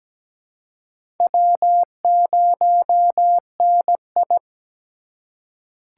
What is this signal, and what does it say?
Morse code, heard as a keyed tone, about 17 wpm, 700 Hz. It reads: W0NI
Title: USA Call Signs - 3x Character Spacing - 17wpm